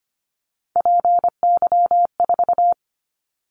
Morse code, heard as a keyed tone, about 25 wpm, 700 Hz. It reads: PY4
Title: Callsign Prefixes for Top 100 DXCC - 25wpm